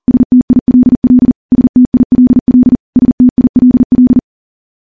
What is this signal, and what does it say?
Morse code, heard as a keyed tone, about 40 wpm, 250 Hz. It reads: STILL STILL STILL
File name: morse / morse-training-40WPM